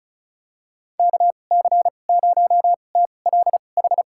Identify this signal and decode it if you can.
Morse code, heard as a keyed tone, about 36 wpm, spaced as for 25 wpm, 700 Hz. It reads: KC0TLH
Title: USA Call Signs - 6 Characters - 2x Character Spacing - 35wpm